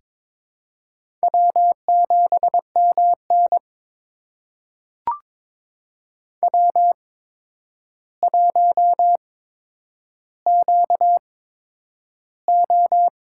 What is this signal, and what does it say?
Morse code, heard as a keyed tone, about 22 wpm, spaced as for 9 wpm, 700 Hz. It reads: W7MN W1QO
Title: USA Call Signs - 4 Characters - 8x Character Spacing - 22wpm